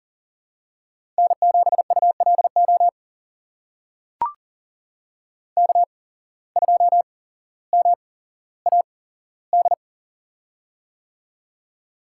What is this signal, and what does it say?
Morse code, heard as a keyed tone, about 37 wpm, spaced as for 17 wpm, 700 Hz. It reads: N7ULO K2MAD
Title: USA Call Signs - 5 Characters - 8x Character Spacing - 40wpm